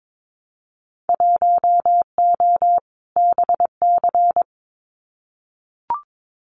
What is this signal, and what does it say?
Morse code, heard as a keyed tone, about 22 wpm, 700 Hz. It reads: 1O BC
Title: Winter Field Day Exchanges Encore 22wpm